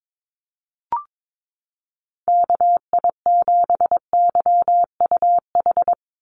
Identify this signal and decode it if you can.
Morse code, heard as a keyed tone, about 22 wpm, 700 Hz. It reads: KI7YUH
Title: USA Call Signs 22wpm